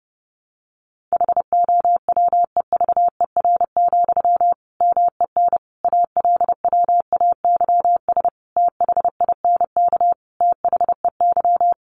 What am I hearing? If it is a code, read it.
Morse code, heard as a keyed tone, about 30 wpm, 700 Hz. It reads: HOWEVER, MEN ALWAYS THINK THEY
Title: Sentences from Top 600 Words 30wpm